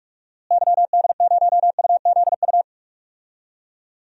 Morse code, heard as a keyed tone, about 46 wpm, 700 Hz. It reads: YD0UZU
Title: DXCC-100 Call Signs - 6 Characters - Repeated 3x - 45wpm